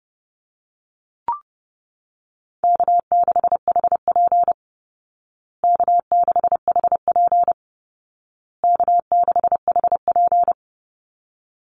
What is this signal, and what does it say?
Morse code, heard as a keyed tone, about 30 wpm, 700 Hz. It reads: K6HP K6HP K6HP
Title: Easy Call Signs Repeated 3x - 30wpm